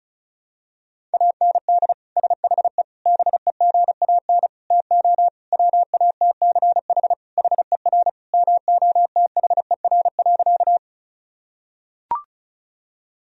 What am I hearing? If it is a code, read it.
Morse code, heard as a keyed tone, about 35 wpm, 700 Hz. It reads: AND SHE BEGAN TO WATCH HER MOTHER.
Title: Sentences from Top 1000 Words - Review Part 1 - 35wpm